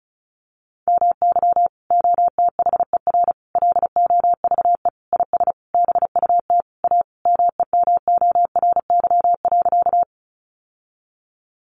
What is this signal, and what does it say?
Morse code, heard as a keyed tone, about 34 wpm, 700 Hz. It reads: MY OTHER LOVE IS BUT A MEMORY.
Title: Sentences from Top 1300 Words 35wpm